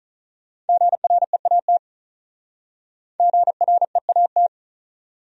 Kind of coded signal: Morse code

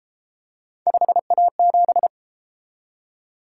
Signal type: Morse code

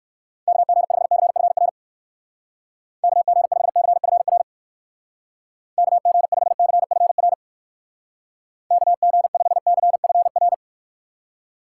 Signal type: Morse code

